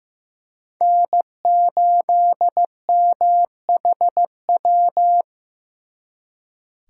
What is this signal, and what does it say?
Morse code, heard as a keyed tone, about 18 wpm, 700 Hz. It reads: N8MHW
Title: Speed Racing - USA 1x3 Call Signs 15wpm